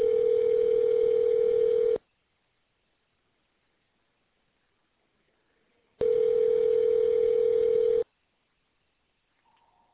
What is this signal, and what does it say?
Ringback tone (North America), 440 + 480 Hz.